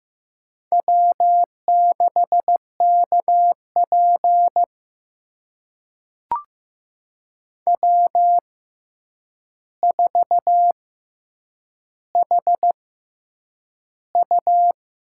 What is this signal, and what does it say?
Morse code, heard as a keyed tone, about 15 wpm, spaced as for 8 wpm, 700 Hz. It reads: W6KP W4HU
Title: USA Call Signs - 6x Character Spacing - 15wpm